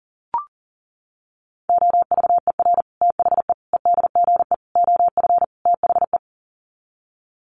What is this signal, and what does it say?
Morse code, heard as a keyed tone, about 40 wpm, 700 Hz. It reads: OVER THE EDGE OF THE
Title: Sets of 5 Words Encore2 40wpm